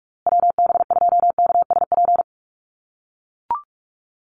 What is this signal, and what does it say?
Morse code, heard as a keyed tone, about 44 wpm, 700 Hz. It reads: WB2KSP
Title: USA Call Signs Repeated 3x 45wpm